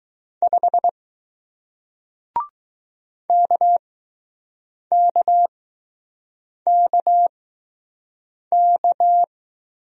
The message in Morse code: 5 KKKK

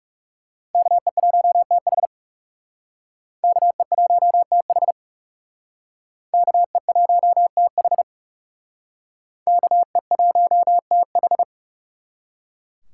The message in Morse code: KE1TH KE1TH KE1TH KE1TH

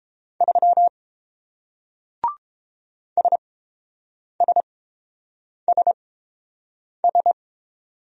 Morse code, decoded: 3 SSSS

